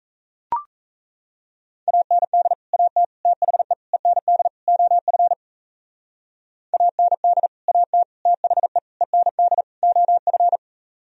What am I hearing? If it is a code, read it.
Morse code, heard as a keyed tone, about 34 wpm, 700 Hz. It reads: AND AT THE END OF AND AT THE END OF
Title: Speed-Racing - Sets of 5 Words 28wpm